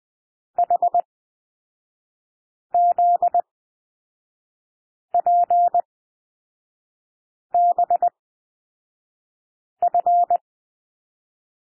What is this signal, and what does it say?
Morse code, heard as a keyed tone, about 20 wpm, spaced as for 5 wpm, 700 Hz. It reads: HZPBF
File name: SOUND LETTERS 20 wpm